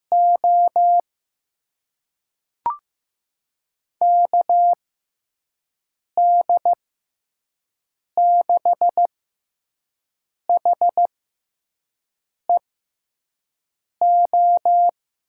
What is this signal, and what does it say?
Morse code, heard as a keyed tone, about 15 wpm, spaced as for 5 wpm, 700 Hz. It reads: O KD6HEO
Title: USA Call Signs - 6 Characters - 6x Character Spacing - 15wpm